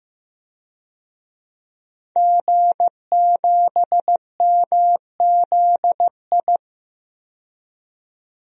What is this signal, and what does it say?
Morse code, heard as a keyed tone, about 15 wpm, 700 Hz. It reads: G7MZI
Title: DXCC-100 Call Signs - 5 Characters - 15wpm